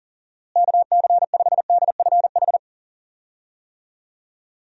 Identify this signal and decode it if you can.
Morse code, heard as a keyed tone, about 40 wpm, 700 Hz. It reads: KC5DFH
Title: USA Call Signs - 6 Characters - Repeated 3x - 40wpm